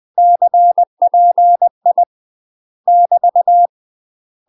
Morse code, heard as a keyed tone, about 20 wpm, 700 Hz. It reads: CPI <BT>
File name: UK MASTER QSOs TRAINING / 20WPM / UK MASTER QSOs 0055